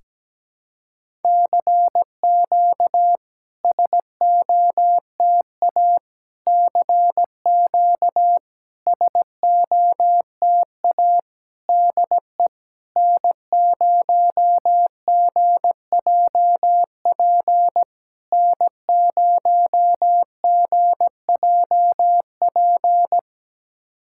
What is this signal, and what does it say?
Morse code, heard as a keyed tone, about 17 wpm, 690 Hz. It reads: CQ SOTA CQ SOTA DE N0GJP N0GJP
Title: SOTA QSO Practice 17wpm